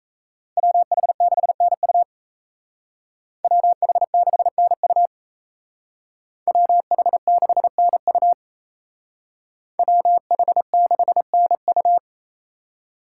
Morse code, heard as a keyed tone, about 36 wpm, 700 Hz. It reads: WH6NU WH6NU WH6NU WH6NU